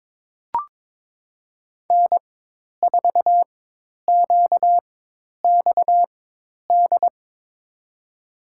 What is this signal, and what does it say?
Morse code, heard as a keyed tone, about 22 wpm, spaced as for 10 wpm, 700 Hz. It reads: N4QXD
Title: USA Call Signs - 5 Characters - 4x Character Spacing - 22wpm